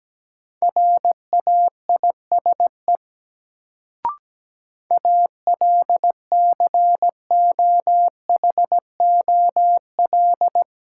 Morse code, heard as a keyed tone, about 17 wpm, 700 Hz. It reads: RAISE ALCOHOL
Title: Sentences from Top 1400 Words - New Word Drill Alt - 17wpm